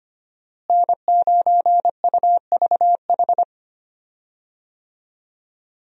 Morse code, heard as a keyed tone, about 25 wpm, 700 Hz. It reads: N9UVH
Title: USA Call Signs - 5 Characters - Repeated 3x - 25wpm